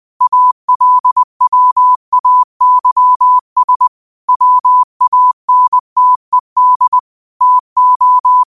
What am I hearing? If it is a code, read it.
Morse code, heard as a keyed tone, about 20 wpm, 1000 Hz. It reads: ALWAYS WANTED TO